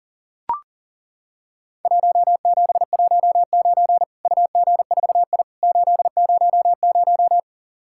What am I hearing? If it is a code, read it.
Morse code, heard as a keyed tone, about 40 wpm, 700 Hz. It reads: 1719 UG4I 800
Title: Worked All Europe DX Contest - QTC Exchange - 40wpm